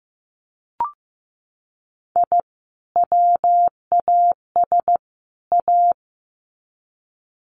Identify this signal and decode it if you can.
Morse code, heard as a keyed tone, about 15 wpm, 700 Hz. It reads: I WAS A
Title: Sets of 3 Words 15wpm